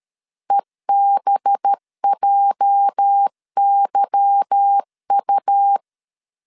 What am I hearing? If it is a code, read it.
Morse code, heard as a keyed tone, about 13 wpm, 800 Hz. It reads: EBJYU
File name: LEZIONI / 8 YJ45 / 8 YJ 12wpm 2235